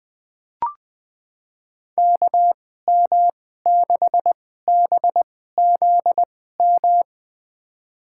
Morse code, heard as a keyed tone, about 20 wpm, spaced as for 14 wpm, 700 Hz. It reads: KM6BZM
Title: USA Call Signs - 6 Characters - 2x Character Spacing - 20wpm